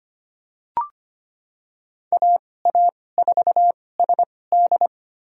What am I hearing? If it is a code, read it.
Morse code, heard as a keyed tone, about 25 wpm, spaced as for 18 wpm, 700 Hz. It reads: AA4SD
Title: USA Call Signs - 5 Characters - 2x Character Spacing - 25wpm